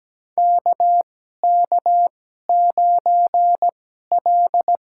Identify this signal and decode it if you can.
Morse code, heard as a keyed tone, about 17 wpm, spaced as for 12 wpm, 700 Hz. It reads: KK9L
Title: USA Call Signs - 2x Character Spacing - 17wpm